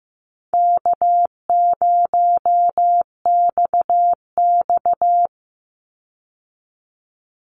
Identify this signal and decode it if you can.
Morse code, heard as a keyed tone, about 15 wpm, 700 Hz. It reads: K0XX
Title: Easy Call Signs Repeated 2x - 15wpm